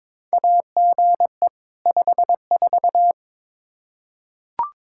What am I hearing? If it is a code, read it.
Morse code, heard as a keyed tone, about 22 wpm, 700 Hz. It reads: AGE 54
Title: Practice QSO - Exchange 3 - More Info - 22wpm